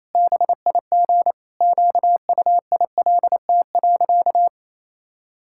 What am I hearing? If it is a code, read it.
Morse code, heard as a keyed tone, about 28 wpm, 700 Hz. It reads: BIG QUILT.
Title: Pangram Challenge v2 - 28wpm